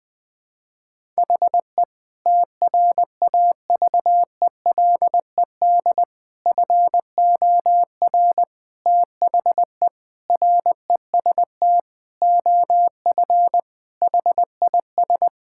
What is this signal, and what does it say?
Morse code, heard as a keyed tone, about 20 wpm, 700 Hz. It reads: HE TRAVELED FOR THE REST OF HIS
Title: Sentences with 5-Word Phrases - 20wpm